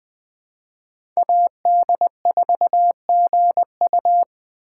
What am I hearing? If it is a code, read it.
Morse code, heard as a keyed tone, about 20 wpm, 700 Hz. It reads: AD4GU
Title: Speed Racing - USA 2x2 Call Signs 17wpm